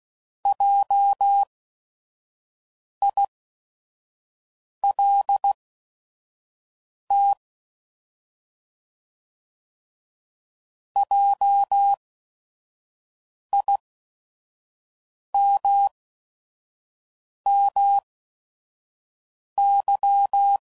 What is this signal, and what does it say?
Morse code, heard as a keyed tone, about 16 wpm, spaced as for 5 wpm, 800 Hz. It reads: JILT JIMMY